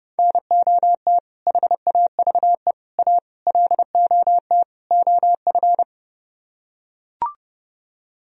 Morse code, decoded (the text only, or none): NOT HAVE A LOT OF